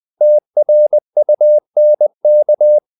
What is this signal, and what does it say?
Morse code, heard as a keyed tone, about 20 wpm, 600 Hz. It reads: TRUNK
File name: the old man and the sea / the old man and the sea part 1 20wpm